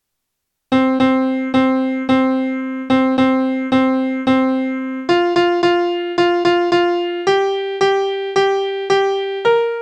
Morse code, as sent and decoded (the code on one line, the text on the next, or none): none